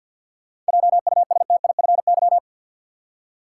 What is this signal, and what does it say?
Morse code, heard as a keyed tone, about 49 wpm, 700 Hz. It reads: JUSTIFY
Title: Seven Letter Words - 50wpm